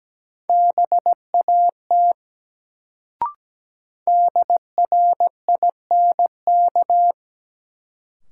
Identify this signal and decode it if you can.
Morse code, heard as a keyed tone, about 17 wpm, 700 Hz. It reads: BAT DRINK
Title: Sentences from Top 1500 Words - New Word Drill Alt - 17wpm